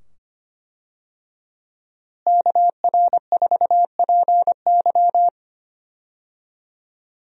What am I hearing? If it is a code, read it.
Morse code, heard as a keyed tone, about 25 wpm, 700 Hz. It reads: KR4PY
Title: USA Call Signs - 5 Characters - Repeated 2x - 25wpm